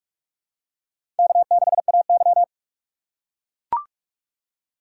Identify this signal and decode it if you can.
Morse code, heard as a keyed tone, about 41 wpm, 690 Hz. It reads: K6AY
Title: USA Call Signs - 4 Characters - 3x Character Spacing - 45wpm